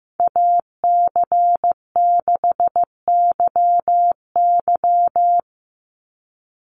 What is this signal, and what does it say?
Morse code, heard as a keyed tone, about 15 wpm, 700 Hz. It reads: AC6YY
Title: Call Signs Repeated 2x - 15wpm